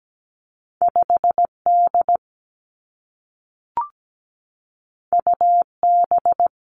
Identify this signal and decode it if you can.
Morse code, heard as a keyed tone, about 17 wpm, 700 Hz. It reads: 5 D U B
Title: Two Challenging Letter-Number Combinations 17wpm